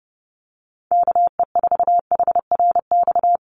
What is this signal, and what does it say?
Morse code, heard as a keyed tone, about 30 wpm, 700 Hz. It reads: KE4HRX
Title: USA Call Signs 30wpm